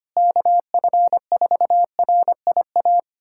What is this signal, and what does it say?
Morse code, heard as a keyed tone, about 25 wpm, 700 Hz. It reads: KF4RIA
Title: USA Callsign DE Callsign 25wpm